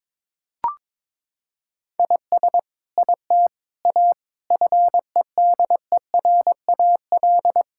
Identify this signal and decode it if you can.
Morse code, heard as a keyed tone, about 22 wpm, 700 Hz. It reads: IS IT A FEDERAL
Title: Sentences from Top 1000 Words - Review Part 5 - 22wpm